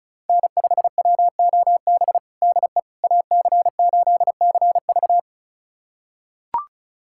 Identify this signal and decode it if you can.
Morse code, heard as a keyed tone, about 36 wpm, 700 Hz. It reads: N5WOB DE AC8CV
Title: USA Callsign DE Callsign - 5 Characters - 35wpm